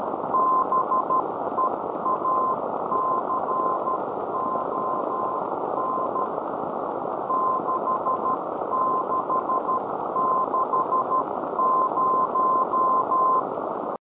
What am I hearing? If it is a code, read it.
Morse code, heard as a keyed tone, about 12 wpm, 1040 Hz, with noise. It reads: BEACON 6660